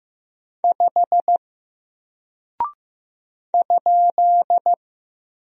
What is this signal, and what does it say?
Morse code, heard as a keyed tone, about 15 wpm, 700 Hz. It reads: 5 ?
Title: Single Character (Letter:Number:Punctuation:Prosign) Alt 15wpm